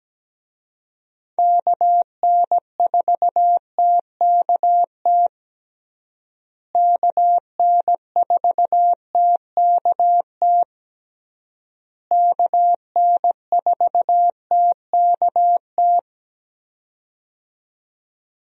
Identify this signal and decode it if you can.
Morse code, heard as a keyed tone, about 17 wpm, 700 Hz. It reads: KN4TKT KN4TKT KN4TKT
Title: USA Call Signs - 6 Characters - Repeated 3x - 17wpm